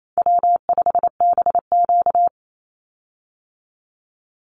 Morse code, heard as a keyed tone, about 28 wpm, 700 Hz. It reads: W5BQ
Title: Easy Call Signs Repeated 3x - 28wpm